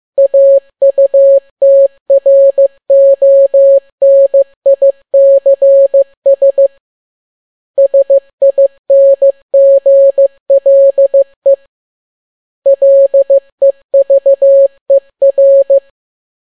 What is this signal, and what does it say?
Morse code, heard as a keyed tone, about 15 wpm, 550 Hz. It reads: AUTRONICS SINGLE LEVER
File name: n6tt p1 15C13